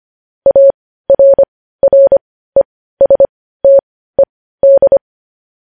Morse code, heard as a keyed tone, about 25 wpm, spaced as for 15 wpm, 550 Hz. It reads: ARRESTED
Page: news-headlines-in-morse-code-at-15-wpm